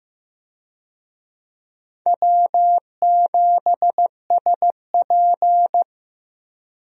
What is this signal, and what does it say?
Morse code, heard as a keyed tone, about 15 wpm, 700 Hz. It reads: W7SP
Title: USA Call Signs - 4 Characters - Repeated 3x - 15wpm